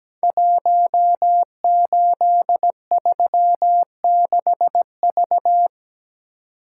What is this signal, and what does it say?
Morse code, heard as a keyed tone, about 17 wpm, 700 Hz. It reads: 1836V
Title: Code Groups - 5 Letter:Numbers - 17wpm